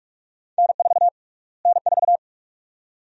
Morse code, heard as a keyed tone, about 46 wpm, 700 Hz. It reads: N4 N4
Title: USA Call Sign Prefixes - Repeated 2x - 45wpm